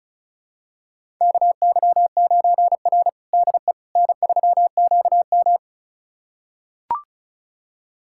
Morse code, heard as a keyed tone, about 36 wpm, 700 Hz. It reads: KY9R DE N3QM
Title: USA Callsign DE Callsign - 4 Characters - 35wpm